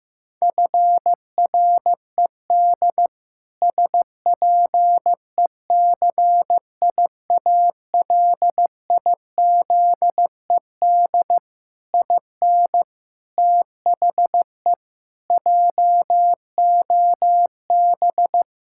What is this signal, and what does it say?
Morse code, heard as a keyed tone, about 15 wpm, 700 Hz. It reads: FRED SPECIALIZED IN THE JOB
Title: Pangram Challenge v2 - 15wpm